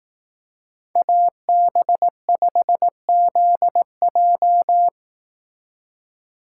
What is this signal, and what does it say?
Morse code, heard as a keyed tone, about 18 wpm, 700 Hz. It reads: AB5ZJ
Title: Speed Racing - USA 2x2 Call Signs 15wpm